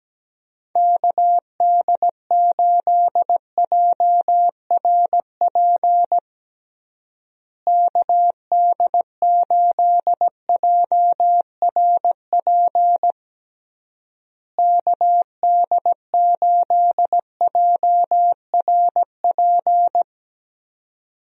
Morse code, heard as a keyed tone, about 17 wpm, 700 Hz. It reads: KD8JRP KD8JRP KD8JRP
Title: USA Call Signs - 6 Characters - Repeated 3x - 17wpm